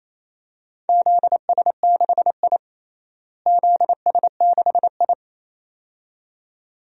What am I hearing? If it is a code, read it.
Morse code, heard as a keyed tone, about 28 wpm, 700 Hz. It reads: ZS6I ZS6I
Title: DXCC-100 Call Signs - 4 Characters - Repeated 2x - 28wpm